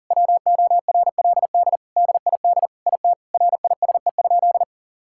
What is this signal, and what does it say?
Morse code, heard as a keyed tone, about 40 wpm, 700 Hz. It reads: WORLD DID IT RISE?
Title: Sentences from Top 1000 Words - Review Part 6 - 40wpm